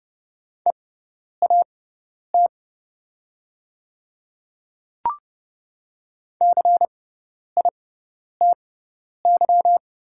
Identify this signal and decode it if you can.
Morse code, heard as a keyed tone, about 30 wpm, spaced as for 10 wpm, 700 Hz. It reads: EAT CITY
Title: 6x Character Spacing - Top 300 Words - 30wpm